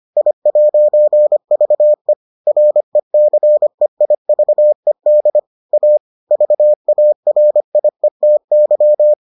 Morse code, heard as a keyed tone, about 25 wpm, 600 Hz. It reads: I'VE RECEIVED A VARIETY